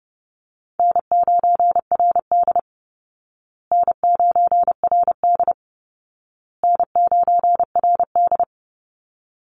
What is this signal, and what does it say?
Morse code, heard as a keyed tone, about 30 wpm, 700 Hz. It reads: N9RD N9RD N9RD
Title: Easy Call Signs Repeated 3x - 30wpm